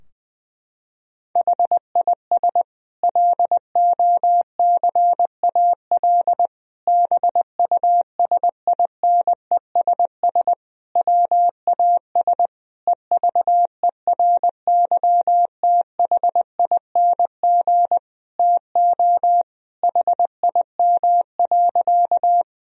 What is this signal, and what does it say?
Morse code, heard as a keyed tone, about 20 wpm, 700 Hz. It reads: HIS LOCAL BUSINESS WAS EVERYTHING TO HIM.
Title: Sentences from Top 1000 Words - Review Part 3 - 20wpm